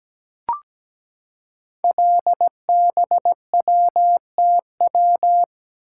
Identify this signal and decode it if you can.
Morse code, heard as a keyed tone, about 17 wpm, 700 Hz. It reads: LBWTW
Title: Code Groups - 5 Letter:Numbers - 17wpm